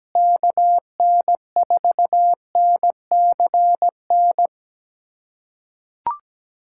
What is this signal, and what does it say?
Morse code, heard as a keyed tone, about 17 wpm, 700 Hz. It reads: KN4NCN
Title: USA Callsign DE Callsign 17wpm